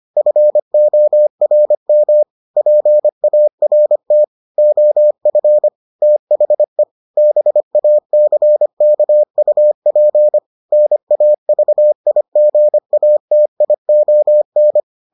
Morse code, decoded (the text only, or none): FORM PART OF THE BACKUP NAVIGATION